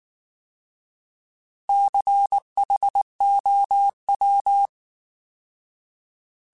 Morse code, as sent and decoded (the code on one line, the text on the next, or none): -.-. .... --- .--
CHOW